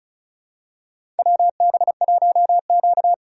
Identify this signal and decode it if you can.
Morse code, heard as a keyed tone, about 36 wpm, 690 Hz. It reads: WB1Q
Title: USA Call Signs - 3x Character Spacing - 35wpm